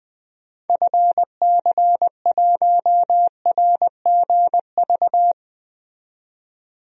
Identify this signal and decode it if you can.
Morse code, heard as a keyed tone, about 21 wpm, 700 Hz. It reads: FC1RGV
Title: DXCC-100 Call Signs - 6 Characters - 20wpm